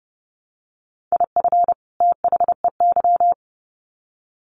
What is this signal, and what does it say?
Morse code, heard as a keyed tone, about 30 wpm, 700 Hz. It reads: IF THEY
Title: Sets of 2 Words 30wpm